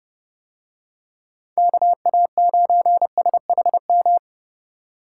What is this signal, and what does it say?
Morse code, heard as a keyed tone, about 30 wpm, 700 Hz. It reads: KA9SHM